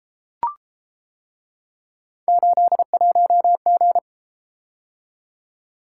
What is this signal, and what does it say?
Morse code, heard as a keyed tone, about 30 wpm, 700 Hz. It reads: 81G